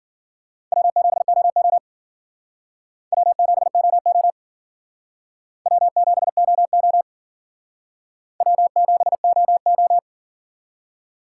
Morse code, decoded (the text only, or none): W7OO W7OO W7OO W7OO